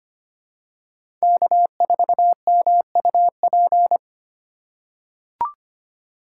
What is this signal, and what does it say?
Morse code, heard as a keyed tone, about 25 wpm, 700 Hz. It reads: K4MUP
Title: USA Call Signs - 5 Characters - 2x Character Spacing - 25wpm